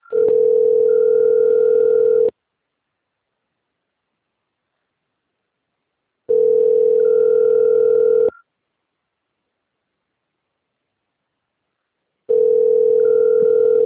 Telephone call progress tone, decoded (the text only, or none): ringback tone